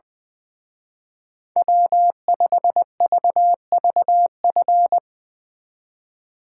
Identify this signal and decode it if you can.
Morse code, heard as a keyed tone, about 20 wpm, 700 Hz. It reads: W5VVF
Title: Speed Racing - USA 1x3 Call Signs 20wpm